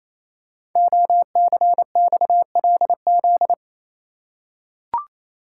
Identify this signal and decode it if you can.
Morse code, heard as a keyed tone, about 29 wpm, 700 Hz. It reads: OCXLZ
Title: Code Groups - 5 Letters - 28wpm